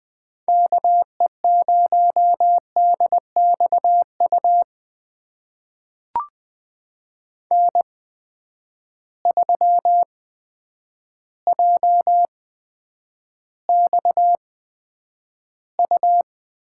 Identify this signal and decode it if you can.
Morse code, heard as a keyed tone, about 20 wpm, spaced as for 9 wpm, 700 Hz. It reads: KE0DXU N3JXU
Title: USA Call Signs - 8x Character Spacing - 20wpm